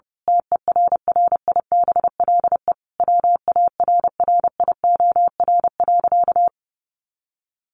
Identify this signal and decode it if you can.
Morse code, heard as a keyed tone, about 30 wpm, 700 Hz. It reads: TERRIBLE WARRIOR.
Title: A Sound Of Thunder 30wpm